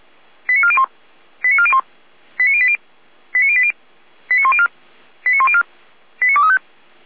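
Five-tone selective call (ZVEI), heard as quick runs of five tones, tones of about 70 ms. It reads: 89401 89401 89080 89080 89104 89104 89235
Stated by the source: ZVEI five-tone selective call received on SDR.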